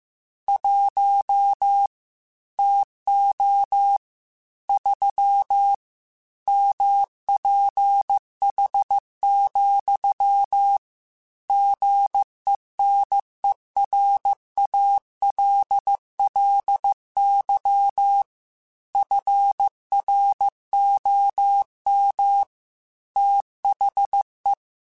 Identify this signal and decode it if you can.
Morse code, heard as a keyed tone, about 15 wpm, 800 Hz. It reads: 1 TO 3 MPH, GENERALLY FROM THE
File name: Weather Sets / Weather Sets-mp3s / Weather 5 1 @13